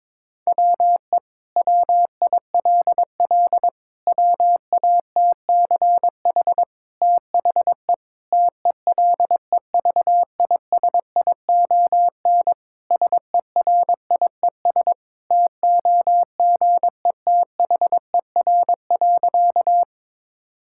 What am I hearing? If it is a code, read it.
Morse code, heard as a keyed tone, about 22 wpm, 700 Hz. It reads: WE WILL WATCH THE TELEVISION SERIES TOGETHER.
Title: Sentences from Top 1000 Words - Review Part 5 - 22wpm